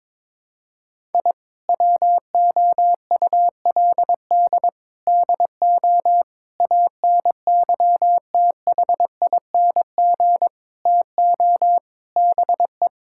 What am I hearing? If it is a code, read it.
Morse code, heard as a keyed tone, about 22 wpm, 700 Hz. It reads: I WOULD DO ANYTHING TO BE
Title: Sentences from Top 1000 Words - Review Part 6 - 22wpm